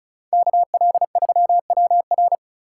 Morse code, heard as a keyed tone, about 34 wpm, 700 Hz. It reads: KL3WR
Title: USA Call Signs - 5 Characters - 2x Character Spacing - 35wpm